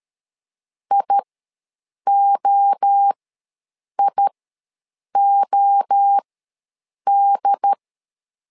Morse code, heard as a keyed tone, about 13 wpm, spaced as for 7 wpm, 800 Hz. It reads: IOIOD
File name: LEZIONI / 3 MWRO / 3 MWRO ws 12wpm 1914